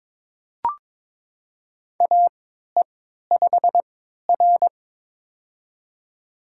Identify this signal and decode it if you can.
Morse code, heard as a keyed tone, about 22 wpm, spaced as for 13 wpm, 700 Hz. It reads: AE5R
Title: USA Call Signs - 4 Characters - 3x Character Spacing - 22wpm